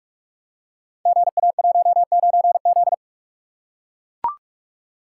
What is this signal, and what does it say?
Morse code, heard as a keyed tone, about 44 wpm, 690 Hz. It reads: GA19Z
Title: Code Groups - 5 Letter:Numbers - 45wpm